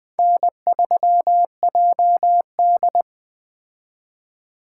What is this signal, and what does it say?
Morse code, heard as a keyed tone, about 20 wpm, 700 Hz. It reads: N3JD
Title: Speed Racing - USA 1x2 Call Signs 17wpm